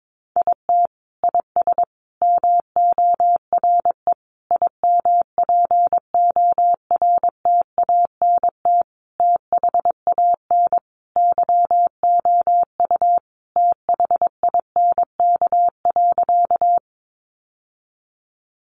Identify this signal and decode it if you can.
Morse code, heard as a keyed tone, about 22 wpm, 700 Hz. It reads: IT IS MORE IMPORTANT THAN YOU THINK.
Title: Sentences from Top 600 Words 22wpm